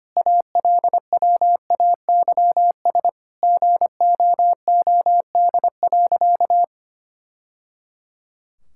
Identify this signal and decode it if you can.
Morse code, heard as a keyed tone, about 25 wpm, 700 Hz. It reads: ALWAYS GOOD.
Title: Sentences from Top 1000 Words - Review Part 5 - 25wpm